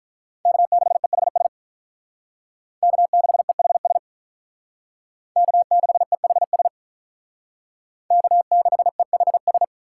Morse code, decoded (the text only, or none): K6EHS K6EHS K6EHS K6EHS